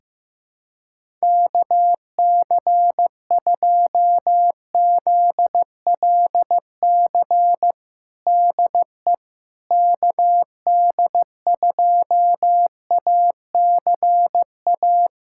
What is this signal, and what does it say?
Morse code, heard as a keyed tone, about 15 wpm, 700 Hz. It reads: KC2ZLC DE KD2ACA
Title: USA Callsign DE Callsign 15wpm